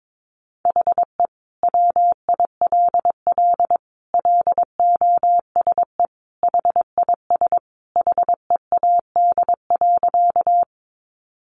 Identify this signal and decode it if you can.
Morse code, heard as a keyed tone, about 22 wpm, 700 Hz. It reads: HE WILL LOSE HIS HEAD.
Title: Sentences from Top 1100 Words 22wpm